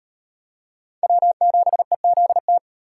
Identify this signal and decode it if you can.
Morse code, heard as a keyed tone, about 38 wpm, 700 Hz. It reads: W7EZT